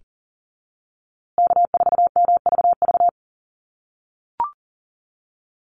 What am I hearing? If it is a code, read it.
Morse code, heard as a keyed tone, about 40 wpm, 700 Hz. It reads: K4MVV
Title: USA Call Signs Repeated 3x 40wpm